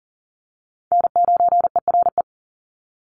Morse code, heard as a keyed tone, about 40 wpm, 700 Hz. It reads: N9ERE
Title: USA Call Signs 40wpm